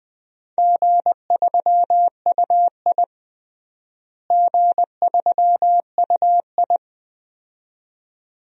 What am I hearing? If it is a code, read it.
Morse code, heard as a keyed tone, about 20 wpm, 700 Hz. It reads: G3UI G3UI
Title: DXCC-100 Call Signs - 4 Characters - Repeated 2x - 20wpm